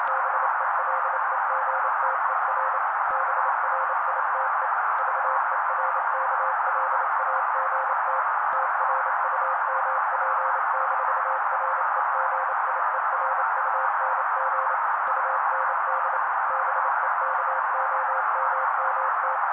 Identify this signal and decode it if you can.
Morse code, heard as a keyed tone, about 28 wpm, 530 Hz. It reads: BELEGTER BRITE VERKLAGT TRUMP-REGIERUNG UND BEKOMMT
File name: daily / 27wpm noise